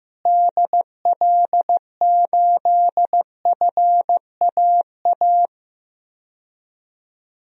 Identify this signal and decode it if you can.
Morse code, heard as a keyed tone, about 15 wpm, 700 Hz. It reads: DL8FAA
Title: DXCC-100 Call Signs - 6 Characters - Repeated 2x - 15wpm